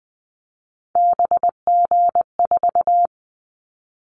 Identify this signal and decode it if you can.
Morse code, heard as a keyed tone, about 20 wpm, 700 Hz. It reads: BG4
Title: Callsign Prefixes for Top 100 DXCC - 20wpm